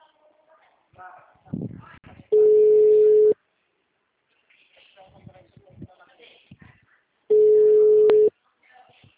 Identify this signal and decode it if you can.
Ringback tone (Europe, 425 Hz), with speech.